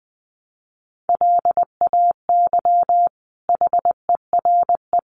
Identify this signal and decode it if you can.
Morse code, heard as a keyed tone, about 20 wpm, 700 Hz. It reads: LAY HERE
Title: Sentences from Top 1100 Words 20wpm